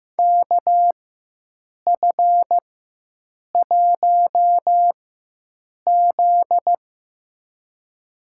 Morse code, heard as a keyed tone, about 15 wpm, spaced as for 7 wpm, 700 Hz. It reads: KF1Z
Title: USA Call Signs - 4x Character Spacing - 15wpm